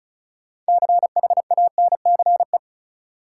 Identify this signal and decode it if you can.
Morse code, heard as a keyed tone, about 36 wpm, 700 Hz. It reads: CHANCE
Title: Sets of 3 Words - New Word Drill - 35wpm